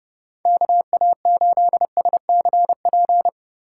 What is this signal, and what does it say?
Morse code, heard as a keyed tone, about 31 wpm, 700 Hz. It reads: KA8SCP